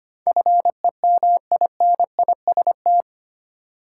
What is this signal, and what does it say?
Morse code, heard as a keyed tone, about 25 wpm, 700 Hz. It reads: FEMINIST
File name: Eight Letter Words - 25wpm